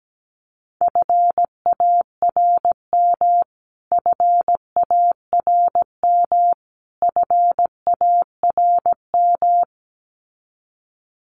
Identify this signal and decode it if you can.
Morse code, heard as a keyed tone, about 17 wpm, 700 Hz. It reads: FARM FARM FARM
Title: Top 500 Words Repeated 17wpm